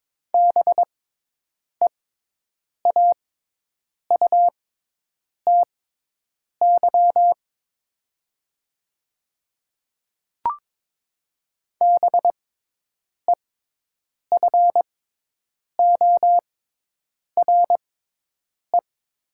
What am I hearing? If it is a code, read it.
Morse code, heard as a keyed tone, about 22 wpm, spaced as for 8 wpm, 700 Hz. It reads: BEAUTY BEFORE
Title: 6x Character Spacing - Top 500 Words - 22wpm